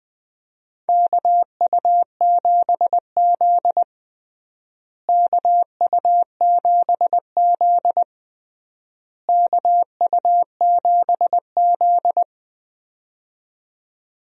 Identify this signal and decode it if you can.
Morse code, heard as a keyed tone, about 21 wpm, 700 Hz. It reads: KU7Z KU7Z KU7Z
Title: USA Call Signs - 4 Characters - Repeated 3x - 20wpm